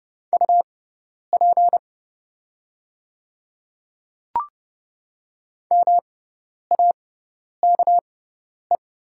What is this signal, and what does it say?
Morse code, heard as a keyed tone, about 29 wpm, spaced as for 10 wpm, 690 Hz. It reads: UP MAKE
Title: 6x Character Spacing - Top 200 Words - 30wpm